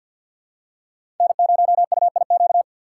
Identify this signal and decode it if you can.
Morse code, heard as a keyed tone, about 48 wpm, 700 Hz. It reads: N0UIQ
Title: USA Call Signs - 5 Characters - 3x Character Spacing - 50wpm